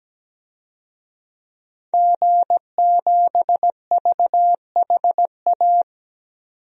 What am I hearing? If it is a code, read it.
Morse code, heard as a keyed tone, about 17 wpm, 700 Hz. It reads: G7VHA
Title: DXCC-100 Call Signs - 5 Characters - 17wpm